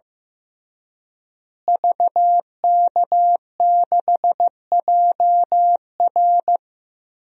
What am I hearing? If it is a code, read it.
Morse code, heard as a keyed tone, about 15 wpm, 700 Hz. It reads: VK6JR
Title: DXCC-100 Call Signs - 5 Characters - Repeated 2x - 15wpm